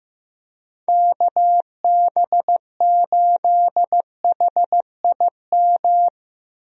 Morse code, heard as a keyed tone, about 15 wpm, 700 Hz. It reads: KB8HIM
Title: USA Call Signs - 6 Characters - 15wpm